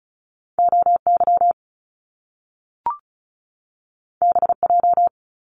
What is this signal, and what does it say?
Morse code, heard as a keyed tone, about 36 wpm, 700 Hz. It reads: OY BJ
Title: Top Two Letter Combinations in Words 35wpm